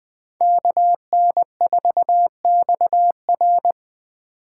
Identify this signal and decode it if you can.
Morse code, heard as a keyed tone, about 23 wpm, 700 Hz. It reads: KN4XR
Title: Speed Racing - USA 2x2 Call Signs 20wpm